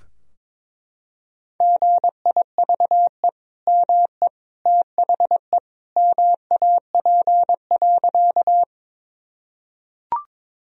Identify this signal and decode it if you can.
Morse code, heard as a keyed tone, about 22 wpm, 700 Hz. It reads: GIVE ME THE MAP.
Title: Sentences from Top 1000 Words - Review Part 2 - 22wpm